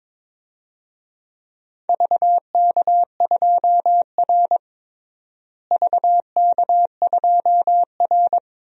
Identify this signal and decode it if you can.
Morse code, heard as a keyed tone, about 22 wpm, 700 Hz. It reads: VK2R VK2R
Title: DXCC-100 Call Signs - 4 Characters - Repeated 3x - 22wpm